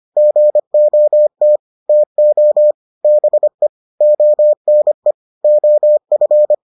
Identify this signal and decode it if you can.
Morse code, heard as a keyed tone, about 25 wpm, 600 Hz. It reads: GOT TO BE ONE OF